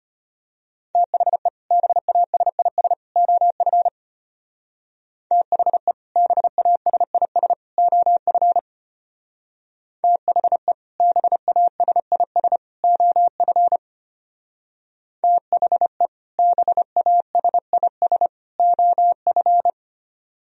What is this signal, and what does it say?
Morse code, heard as a keyed tone, about 29 wpm, 700 Hz. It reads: THE BASIS OF THE BASIS OF THE BASIS OF THE BASIS OF
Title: Speed-Racing - Sets of 3 Words 25wpm